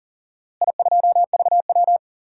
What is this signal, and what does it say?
Morse code, heard as a keyed tone, about 38 wpm, 700 Hz. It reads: I2VW
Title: DXCC-100 Call Signs - 4 Characters - Repeated 3x - 40wpm